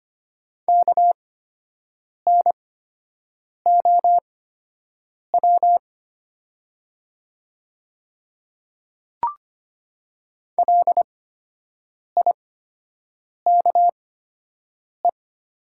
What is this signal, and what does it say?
Morse code, heard as a keyed tone, about 25 wpm, spaced as for 7 wpm, 700 Hz. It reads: KNOW LIKE
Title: 8x Character Spacing - Top 100 Words - 25wpm